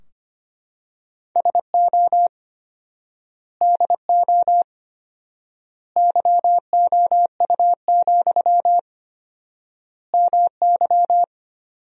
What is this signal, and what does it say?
Morse code, heard as a keyed tone, about 25 wpm, 700 Hz. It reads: SO DO YOU, MY
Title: 4x Word-Spacing - Sentences from Top 400 Words - 25wpm